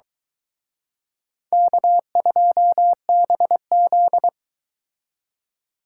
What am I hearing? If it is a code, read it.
Morse code, heard as a keyed tone, about 22 wpm, 700 Hz. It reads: K2BZ